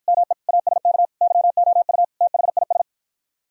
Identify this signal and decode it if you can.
Morse code, heard as a keyed tone, about 53 wpm, 700 Hz. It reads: ME ASK YOU THIS